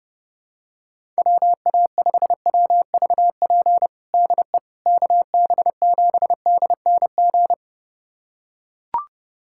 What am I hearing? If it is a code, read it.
Morse code, heard as a keyed tone, about 30 wpm, 700 Hz. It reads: WA5WVP DE KB7DNG
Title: USA Callsign DE Callsign - 6 Characters - 30wpm